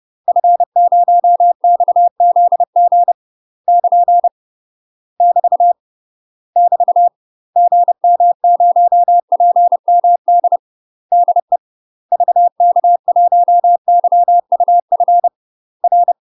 Morse code, heard as a keyed tone, about 30 wpm, 700 Hz. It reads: F0XZG <KN> <BT> <BT> GM0PMD DE VK1YUF R
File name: UK MASTER QSOs TRAINING / 30WPM / UK MASTER QSOs 0015